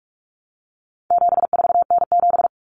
Morse code, heard as a keyed tone, about 44 wpm, 700 Hz. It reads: 74N7
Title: Common Cut Numbers - 4-Digit 45wpm